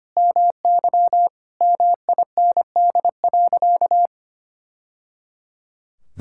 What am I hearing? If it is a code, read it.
Morse code, heard as a keyed tone, about 25 wpm, 700 Hz. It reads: MY MIND.
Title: Sentences from Top 1000 Words - Review Part 6 - 25wpm